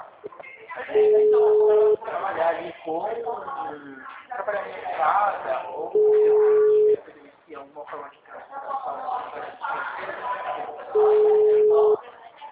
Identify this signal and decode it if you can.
Ringback tone (Europe, 425 Hz), with speech.